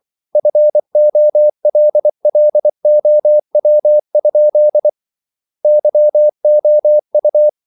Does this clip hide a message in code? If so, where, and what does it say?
Yes: Morse code from 0.4 to 7.5 s, heard as a keyed tone, about 24 wpm, 600 Hz. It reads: FOLLOW? YOU